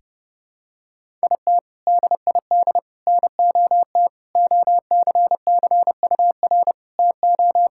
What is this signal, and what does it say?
Morse code, heard as a keyed tone, about 32 wpm, 700 Hz. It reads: IT DID NOT OCCUR TO
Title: Sentences from Top 1000 Words - Review Part 6 - 30wpm